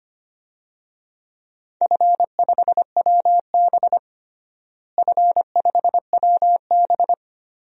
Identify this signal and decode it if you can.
Morse code, heard as a keyed tone, about 25 wpm, 700 Hz. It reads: F5WB F5WB
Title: DXCC-100 Call Signs - 4 Characters - Repeated 3x - 25wpm